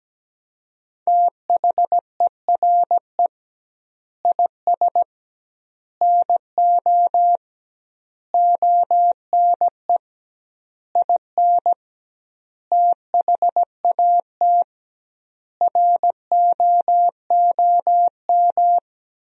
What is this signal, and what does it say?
Morse code, heard as a keyed tone, about 17 wpm, 700 Hz. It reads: THERE IS NO ONE IN THAT ROOM
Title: 2x Word-Spacing - Sentences from Top 500 Words - 17wpm